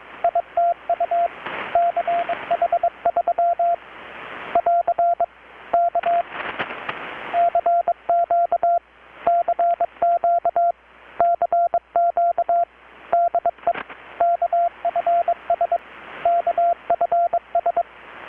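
Morse code, heard as a keyed tone, about 22 wpm, 690 Hz, with radio noise. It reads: ITU CH3 <AR> K CQ CQ CQ DE KFS KFS
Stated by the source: Morse from KPH's Night of Nights XX maritime radiotelegraphy event, received in Chiapas, Mexico about 3,800 km away on a Kenwood R-600 with a grounded 10 m long-wire antenna.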